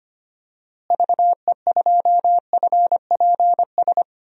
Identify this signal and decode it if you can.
Morse code, heard as a keyed tone, about 25 wpm, 700 Hz. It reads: VE2FPS
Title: DXCC-100 Call Signs - 6 Characters - Repeated 3x - 25wpm